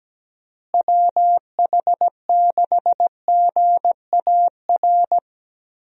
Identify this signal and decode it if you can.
Morse code, heard as a keyed tone, about 17 wpm, 700 Hz. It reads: WH6GAR
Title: USA Call Signs - 6 Characters - Repeated 3x - 17wpm